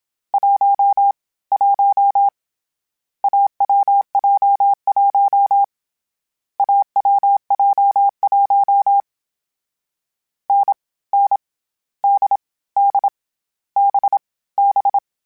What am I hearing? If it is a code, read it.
Morse code, heard as a keyed tone, about 27 wpm, 800 Hz. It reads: 1 1 AWJ1 AWJ1 N N D D B B
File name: Difficult Combos / Difficult Combos-mp3s / awj1 ndb6 gz7 @25